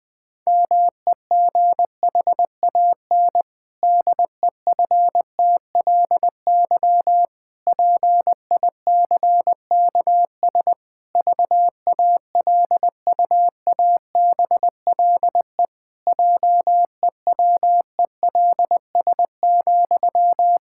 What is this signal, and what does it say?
Morse code, heard as a keyed tone, about 20 wpm, 700 Hz. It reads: MEGHAN DEFTLY PICKS VALUABLE JEWELS,
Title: Pangram Challenge v2 - 20wpm